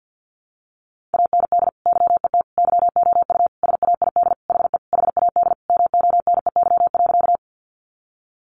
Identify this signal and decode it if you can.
Morse code, heard as a keyed tone, about 50 wpm, 700 Hz. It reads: AND YET YOU SAID HE HAD MONEY.
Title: Sentences from Top 500 Words 50wpm